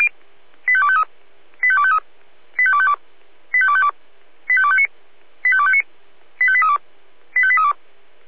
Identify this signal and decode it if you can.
Five-tone selective call (ZVEI), heard as quick runs of five tones, tones of about 70 ms. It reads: 85253 85253 85252 85252 85259 85259 85832 85832